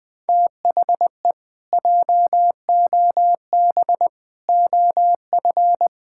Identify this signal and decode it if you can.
Morse code, heard as a keyed tone, about 20 wpm, 700 Hz. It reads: THE JOB OF
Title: Pangram Challenge v2 - 20wpm